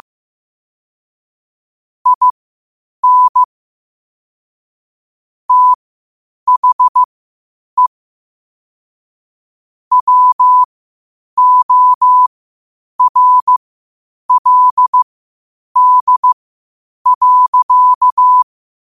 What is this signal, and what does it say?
Morse code, heard as a keyed tone, about 15 wpm, spaced as for 8 wpm, 1000 Hz. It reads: IN THE WORLD.
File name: qotdfiles / 8wpm